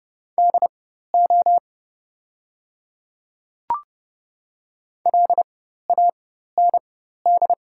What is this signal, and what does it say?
Morse code, heard as a keyed tone, about 30 wpm, spaced as for 14 wpm, 700 Hz. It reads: DO LAND
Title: 4x Character Spacing - Top 200 Words - 30wpm